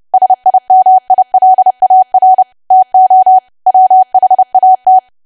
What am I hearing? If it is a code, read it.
Morse code, heard as a keyed tone, about 30 wpm, 750 Hz. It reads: SIMILAR TO WHAT
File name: cwa / W1AW-Level3 / 30WPM